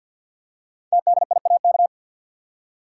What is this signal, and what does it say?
Morse code, heard as a keyed tone, about 48 wpm, 700 Hz. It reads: TDIAK
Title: Code Groups - 5 Letters - 50wpm